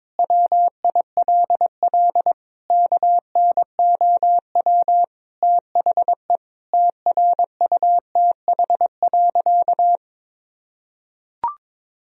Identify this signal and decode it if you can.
Morse code, heard as a keyed tone, about 22 wpm, 700 Hz. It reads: WILL KNOW THE TRUTH.
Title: Sentences from Top 1000 Words - Review Part 3 - 22wpm